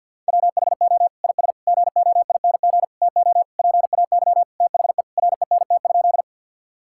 Morse code, heard as a keyed tone, about 50 wpm, 700 Hz. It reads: WHO IS GOING TO PAY THE RENT?
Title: Sentences from Top 1000 Words - Review Part 5 - 50wpm